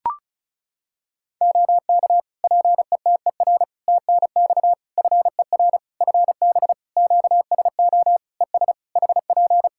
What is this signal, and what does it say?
Morse code, heard as a keyed tone, about 34 wpm, 700 Hz. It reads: OK PETER TNX FER FB QSO ES HP
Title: K1USN SST QSO Practice 35wpm